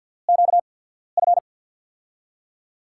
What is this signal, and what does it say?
Morse code, heard as a keyed tone, about 50 wpm, spaced as for 14 wpm, 700 Hz. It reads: QF
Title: USA Call Signs - 6 Characters - 8x Character Spacing - 50wpm